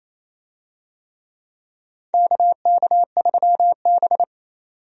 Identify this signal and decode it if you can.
Morse code, heard as a keyed tone, about 28 wpm, 700 Hz. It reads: KK3B
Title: USA Call Signs - 4 Characters - 28wpm